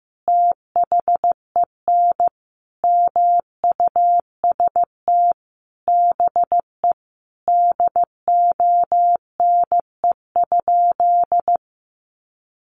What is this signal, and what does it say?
Morse code, heard as a keyed tone, about 15 wpm, 700 Hz. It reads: THEN MUST BE DONE?
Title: Sentences from Top 500 Words 15wpm